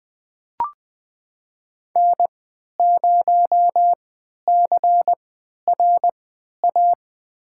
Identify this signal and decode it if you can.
Morse code, heard as a keyed tone, about 20 wpm, spaced as for 11 wpm, 700 Hz. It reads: N0CRA
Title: USA Call Signs - 3x Character Spacing - 20wpm